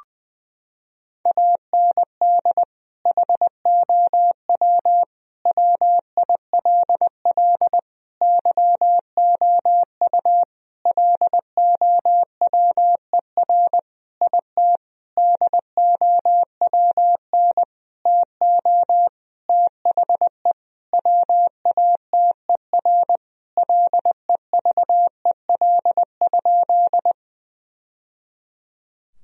Morse code, heard as a keyed tone, about 20 wpm, 700 Hz. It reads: AND HOW WILL YOU LOWER IT DOWN TO THE WATER LEVEL?
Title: Sentences from Top 1000 Words - Review Part 4 - 20wpm